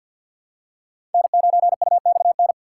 Morse code, heard as a keyed tone, about 52 wpm, 700 Hz. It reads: N9UXN
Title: USA Call Signs - 5 Characters - 2x Character Spacing - 50wpm